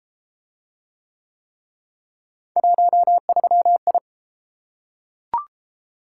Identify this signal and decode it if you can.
Morse code, heard as a keyed tone, about 28 wpm, 700 Hz. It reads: 13I